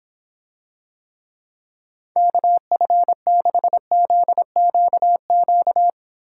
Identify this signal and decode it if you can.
Morse code, heard as a keyed tone, about 26 wpm, 700 Hz. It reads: KF6ZQQ